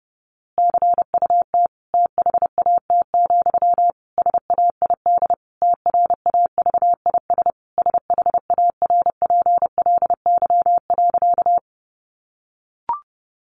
Morse code, heard as a keyed tone, about 30 wpm, 700 Hz. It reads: CUT THAT, SAID TRAVIS SHARPLY.
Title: A Sound Of Thunder 30wpm